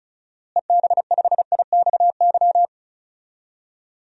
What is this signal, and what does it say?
Morse code, heard as a keyed tone, about 34 wpm, 700 Hz. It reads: EB5IXY
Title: DXCC-100 Call Signs - 6 Characters - Repeated 3x - 35wpm